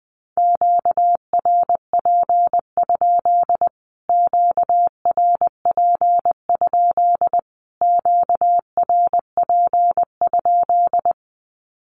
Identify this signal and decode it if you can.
Morse code, heard as a keyed tone, about 20 wpm, 700 Hz. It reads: QRP? QRP? QRP?
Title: Q-Codes Repeated 3x 20wpm